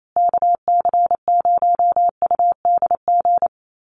Morse code, heard as a keyed tone, about 28 wpm, 700 Hz. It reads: KC0UDG
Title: USA Call Signs Repeated 3x 28wpm